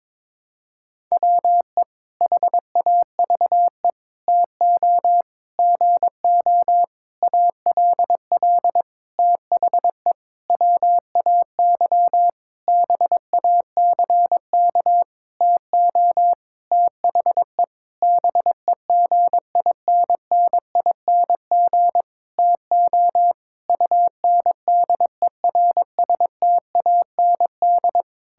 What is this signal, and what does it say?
Morse code, heard as a keyed tone, about 22 wpm, 700 Hz. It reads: WE HAVE TO GO ALL THE WAY BACK TO THE BEGINNING TO UNDERSTAND
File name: practice / 22wpm / Sentences with 8-Word Phrases - 22wpm